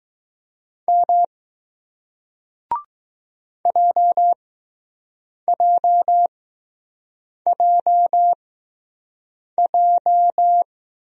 Morse code, heard as a keyed tone, about 20 wpm, spaced as for 6 wpm, 700 Hz. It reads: M JJJJ